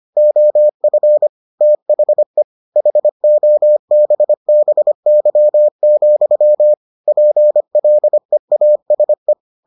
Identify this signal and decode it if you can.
Morse code, heard as a keyed tone, about 25 wpm, 600 Hz. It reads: OF THE HOBBY, PLEASE